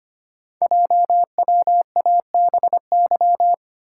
Morse code, heard as a keyed tone, about 25 wpm, 700 Hz. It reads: JWABY
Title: Code Groups - 5 Letters - 25wpm